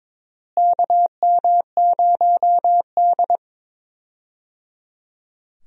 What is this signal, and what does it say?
Morse code, heard as a keyed tone, about 22 wpm, 700 Hz. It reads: KM0D
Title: USA Call Signs - 4 Characters - Repeated 3x - 22wpm